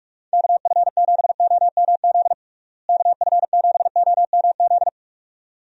Morse code, heard as a keyed tone, about 46 wpm, 700 Hz. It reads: KF7OMZ KF7OMZ
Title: USA Call Signs - 6 Characters - Repeated 2x - 45wpm